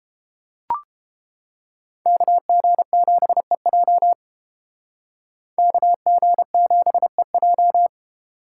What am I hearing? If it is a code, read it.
Morse code, heard as a keyed tone, about 31 wpm, 700 Hz. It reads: KG7EJ KG7EJ